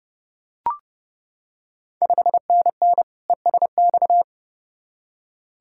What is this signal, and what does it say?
Morse code, heard as a keyed tone, about 31 wpm, 700 Hz. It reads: 5NN ESX
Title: Ontario QSO Party Exchange 30wpm